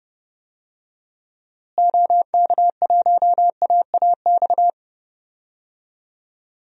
Morse code, heard as a keyed tone, about 31 wpm, 700 Hz. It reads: OK1AAX
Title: DXCC-100 Call Signs - 6 Characters - Repeated 2x - 30wpm